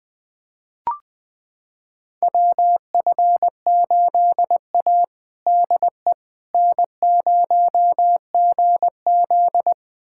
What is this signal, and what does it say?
Morse code, heard as a keyed tone, about 20 wpm, 700 Hz. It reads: WF8A DE N0GZ
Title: USA Callsign DE Callsign - 4 Characters - 20wpm